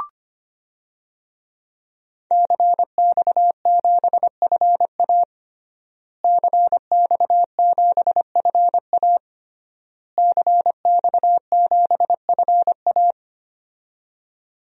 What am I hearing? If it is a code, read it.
Morse code, heard as a keyed tone, about 25 wpm, 700 Hz. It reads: CX7FA CX7FA CX7FA
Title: DXCC-100 Call Signs - 5 Characters - Repeated 3x - 25wpm